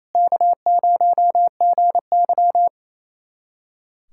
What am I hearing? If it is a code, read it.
Morse code, heard as a keyed tone, about 27 wpm, 700 Hz. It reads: K0GY
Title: USA Callsign DE Callsign - 4 Characters - 28wpm